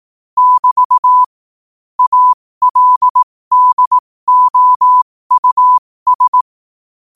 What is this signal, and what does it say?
Morse code, heard as a keyed tone, about 18 wpm, spaced as for 15 wpm, 1000 Hz. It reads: <BT> ALDOUS
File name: qotdfiles / 15wpm